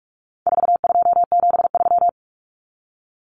Morse code, heard as a keyed tone, about 44 wpm, 700 Hz. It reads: 4273
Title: Numbers - 4-Digit 45wpm